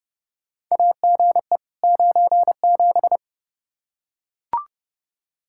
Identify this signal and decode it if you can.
Morse code, heard as a keyed tone, about 30 wpm, 700 Hz. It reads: AGE 97
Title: Practice QSO - Exchange 3 - More Info - 30wpm